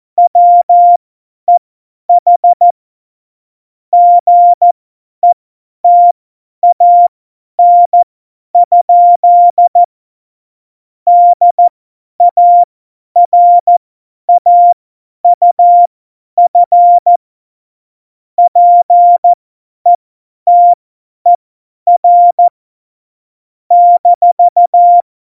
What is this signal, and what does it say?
Morse code, heard as a keyed tone, about 14 wpm, spaced as for 10 wpm, 700 Hz. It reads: WEH GETAN? DARAUF PETER -
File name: Morsekurs / Witze / Witz 26. Dezember 2017, 50 F70